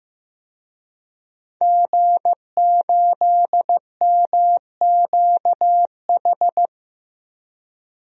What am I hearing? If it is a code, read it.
Morse code, heard as a keyed tone, about 15 wpm, 700 Hz. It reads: G8MQH
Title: DXCC-100 Call Signs - 5 Characters - Repeated 3x - 15wpm